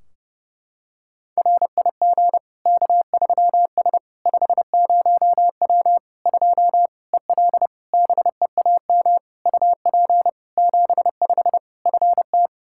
Morse code, heard as a keyed tone, about 30 wpm, 700 Hz. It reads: RIG K3S 50W 2 EL BEAM UP 75 FT
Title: Practice QSO - Exchange 2 - Info - 30wpm